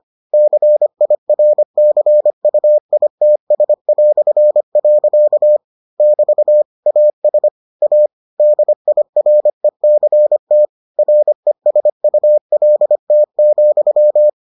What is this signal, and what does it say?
Morse code, heard as a keyed tone, about 25 wpm, 600 Hz. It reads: CIRCUITS". <BT> AS A DIRECT RESULT,